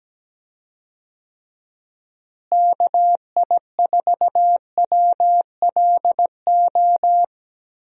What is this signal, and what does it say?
Morse code, heard as a keyed tone, about 17 wpm, 700 Hz. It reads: KI4WLO
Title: USA Call Signs - 6 Characters - Repeated 2x - 17wpm